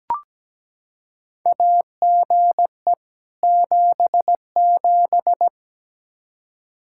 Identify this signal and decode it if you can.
Morse code, heard as a keyed tone, about 17 wpm, 700 Hz. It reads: AGE 77
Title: Practice QSO - Exchange 3 - More Info - 17wpm